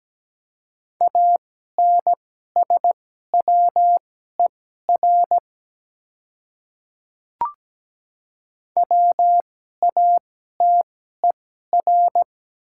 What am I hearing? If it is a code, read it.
Morse code, heard as a keyed tone, about 17 wpm, spaced as for 12 wpm, 700 Hz. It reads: ANSWER WATER
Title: 2x Character Spacing - Top 300 Words - 17wpm